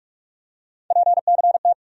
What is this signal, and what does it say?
Morse code, heard as a keyed tone, about 43 wpm, 700 Hz. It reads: PCT
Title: USA Call Sign Postfixes - 3 Letters 45wpm